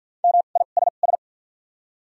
Morse code, heard as a keyed tone, about 48 wpm, spaced as for 35 wpm, 700 Hz. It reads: MISS
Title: 2x Character Spacing - Top 500 Words - 50wpm